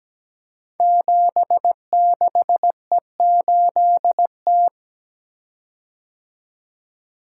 Morse code, heard as a keyed tone, about 17 wpm, 700 Hz. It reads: 76E8T
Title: Code Groups - 5 Letter:Numbers - 17wpm